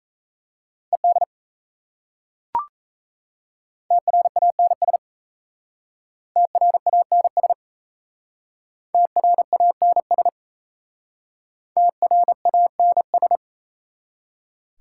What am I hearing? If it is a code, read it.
Morse code, heard as a keyed tone, about 37 wpm, 690 Hz. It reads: ED TRANS TRANS TRANS TRANS